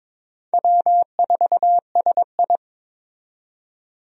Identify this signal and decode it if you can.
Morse code, heard as a keyed tone, about 22 wpm, 700 Hz. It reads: W4SI
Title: USA Call Signs - 4 Characters - 22wpm